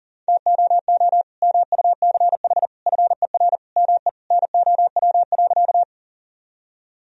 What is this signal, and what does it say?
Morse code, heard as a keyed tone, about 40 wpm, 700 Hz. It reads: TOO MUCH FER ME NOW.
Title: Sentences from Top 200 Words and Abbreviations 40wpm